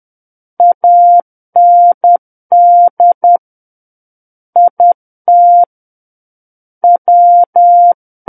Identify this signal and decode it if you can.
Morse code, heard as a keyed tone, about 10 wpm, 700 Hz. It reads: AND IT W
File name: Book One / 01-Chapter 1 - 10 WPM